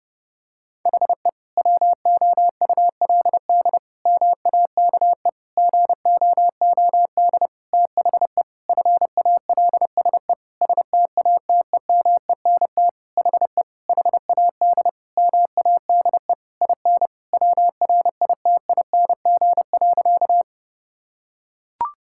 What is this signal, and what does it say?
Morse code, heard as a keyed tone, about 30 wpm, 700 Hz. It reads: HE WOULD MAKE GOOD THE FALSE STATEMENT HE HAD MADE IN WRITING.
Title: Sentences from Top 1600 Words 30wpm